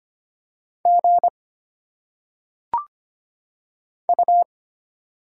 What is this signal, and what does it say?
Morse code, heard as a keyed tone, about 25 wpm, 700 Hz. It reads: G U
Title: Single Letters - Phonetic Pronunciation 25wpm